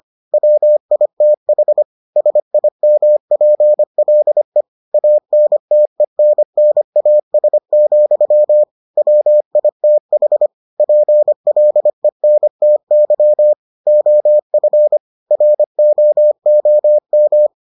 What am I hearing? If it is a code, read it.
Morse code, heard as a keyed tone, about 25 wpm, 600 Hz. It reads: WITH SIMPLE ANTENNAS, WITH PLENTY OF ROOM